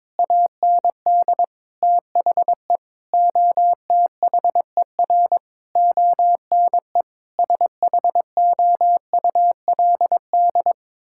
Morse code, heard as a keyed tone, about 22 wpm, 700 Hz. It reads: AND THE OTHER ONE SHOULD
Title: Sentences from Top 1600 Words 22wpm